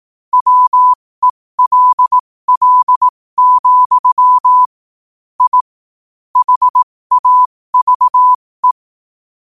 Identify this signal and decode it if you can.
Morse code, heard as a keyed tone, about 18 wpm, spaced as for 15 wpm, 1000 Hz. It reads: WELL, I HAVE
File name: qotdfiles / 15wpm